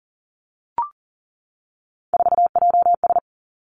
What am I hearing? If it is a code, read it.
Morse code, heard as a keyed tone, about 41 wpm, 700 Hz. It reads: 4JS
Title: Three Challenging Letter-Number Combinations 40wpm